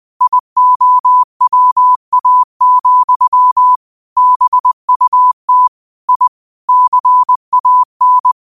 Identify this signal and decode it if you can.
Morse code, heard as a keyed tone, about 20 wpm, 1000 Hz. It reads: IOWA, BUT I CAN